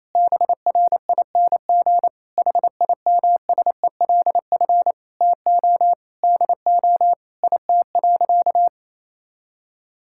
Morse code, heard as a keyed tone, about 28 wpm, 700 Hz. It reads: BRING HIMSELF TO DO IT.
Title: Sentences from Top 1600 Words 28wpm